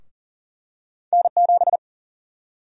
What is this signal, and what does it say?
Morse code, heard as a keyed tone, about 40 wpm, 700 Hz. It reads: N7
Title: USA Call Sign Prefixes - 40wpm